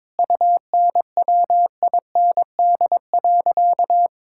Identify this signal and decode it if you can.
Morse code, heard as a keyed tone, about 22 wpm, 700 Hz. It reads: UNWIND.
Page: morse-code-ninja-practice-22wpm